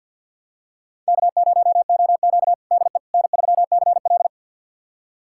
Morse code, heard as a keyed tone, about 48 wpm, 700 Hz. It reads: K0OQ DE N3CL
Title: USA Callsign DE Callsign - 4 Characters - 50wpm